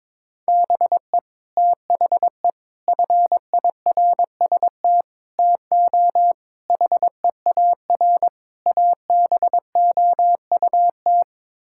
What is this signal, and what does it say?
Morse code, heard as a keyed tone, about 22 wpm, 700 Hz. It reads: BE THE FIRST TO HEAR ABOUT
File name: practice / 22wpm / Sets of 6 Words - 22wpm